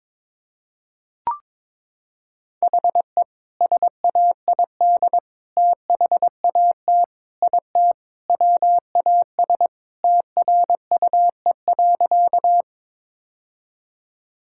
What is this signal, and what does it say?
Morse code, heard as a keyed tone, about 22 wpm, 700 Hz. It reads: HE SAID THAT IT WAS TRUE.
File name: practice / 22wpm / Sentences with 2-Word Phrases - 22wpm